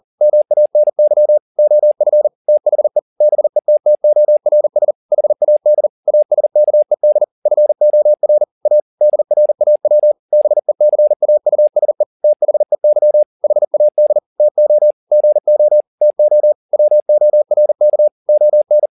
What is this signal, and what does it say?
Morse code, heard as a keyed tone, about 40 wpm, 600 Hz. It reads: MANY OF THE BETTORS HAD ASKED FOR A DRAW BECAUSE THEY HAD TO GO TO WORK ON
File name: the old man and the sea 2 40wpm